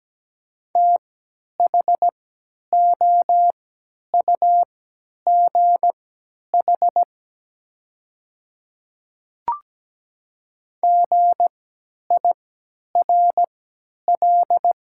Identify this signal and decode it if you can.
Morse code, heard as a keyed tone, about 17 wpm, spaced as for 10 wpm, 690 Hz. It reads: THOUGH GIRL
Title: 3x Character Spacing - Top 500 Words - 17wpm